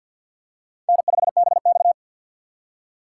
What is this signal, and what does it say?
Morse code, heard as a keyed tone, about 50 wpm, 700 Hz. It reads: N5BX
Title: USA Call Signs - 4x Character Spacing - 50wpm